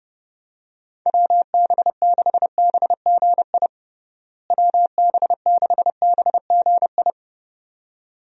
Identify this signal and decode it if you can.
Morse code, heard as a keyed tone, about 30 wpm, 700 Hz. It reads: WB6BGI WB6BGI
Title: USA Call Signs - 6 Characters - Repeated 2x - 30wpm